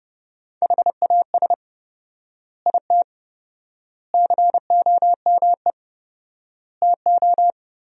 Morse code, heard as a keyed tone, about 30 wpm, 700 Hz. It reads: HAS IT COME TO
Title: 4x Word-Spacing - Sentences from Top 100 Words - 30wpm